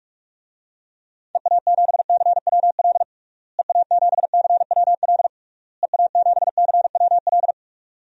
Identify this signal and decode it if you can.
Morse code, heard as a keyed tone, about 46 wpm, 700 Hz. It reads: EA7CWL EA7CWL EA7CWL
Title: DXCC-100 Call Signs - 6 Characters - Repeated 3x - 45wpm